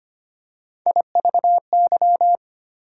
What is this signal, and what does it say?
Morse code, heard as a keyed tone, about 25 wpm, 700 Hz. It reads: IVY
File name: practice / 25wpm / Difficult Words with Hints - 25wpm